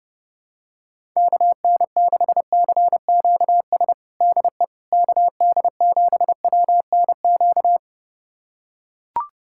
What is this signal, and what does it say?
Morse code, heard as a keyed tone, about 32 wpm, 700 Hz. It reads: KN6CQS DE KD7WNQ
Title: USA Callsign DE Callsign - 6 Characters - 30wpm